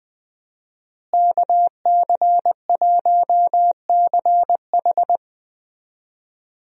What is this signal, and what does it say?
Morse code, heard as a keyed tone, about 20 wpm, 700 Hz. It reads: KC1CH
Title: USA Call Signs - 5 Characters - Repeated 3x - 20wpm